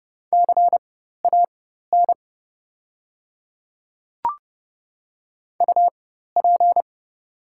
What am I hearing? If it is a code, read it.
Morse code, heard as a keyed tone, about 30 wpm, spaced as for 14 wpm, 690 Hz. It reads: CAN UP
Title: 4x Character Spacing - Top 100 Words - 30wpm